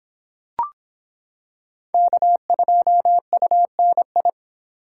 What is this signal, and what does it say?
Morse code, heard as a keyed tone, about 25 wpm, 700 Hz. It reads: K2UNI